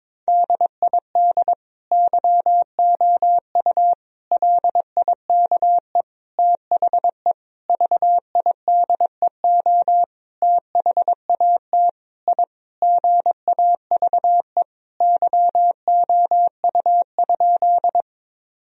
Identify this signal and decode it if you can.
Morse code, heard as a keyed tone, about 22 wpm, 700 Hz. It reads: DID YOU LIKE THE VIDEO THAT I GAVE YOU?
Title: Sentences from Top 1000 Words - Review Part 4 - 22wpm